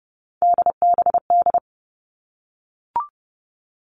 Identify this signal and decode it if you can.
Morse code, heard as a keyed tone, about 30 wpm, 700 Hz. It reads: DBD
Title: All Cut Numbers - 3-Digit 30wpm